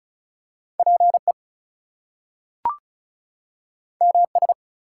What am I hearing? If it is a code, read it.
Morse code, heard as a keyed tone, about 36 wpm, 700 Hz. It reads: PE MS
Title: US States and Canadian Provinces 35wpm